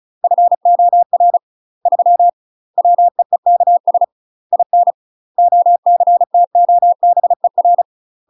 Morse code, heard as a keyed tone, about 36 wpm, 700 Hz. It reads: FOR 3 WEEKS IN OCTOBER
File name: UK MASTER QSOs TRAINING / 35WPM / UK MASTER QSOs 0022